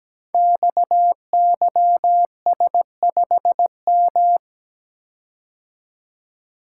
Morse code, heard as a keyed tone, about 17 wpm, 700 Hz. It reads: XYS5M
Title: Code Groups - 5 Letter:Numbers - 17wpm